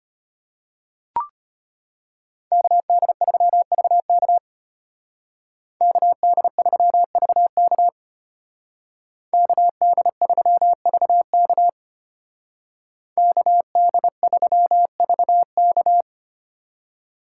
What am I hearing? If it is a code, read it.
Morse code, heard as a keyed tone, about 33 wpm, 700 Hz. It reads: KD3VK KD3VK KD3VK KD3VK